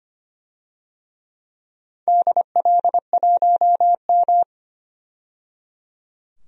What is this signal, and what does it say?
Morse code, heard as a keyed tone, about 25 wpm, 700 Hz. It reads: DL1M
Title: DXCC-100 Call Signs - 4 Characters - 25wpm